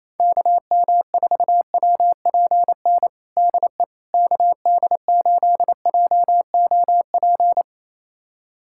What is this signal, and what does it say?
Morse code, heard as a keyed tone, about 28 wpm, 700 Hz. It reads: KM4WPN DE KD8JOP
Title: USA Callsign DE Callsign - 6 Characters - 28wpm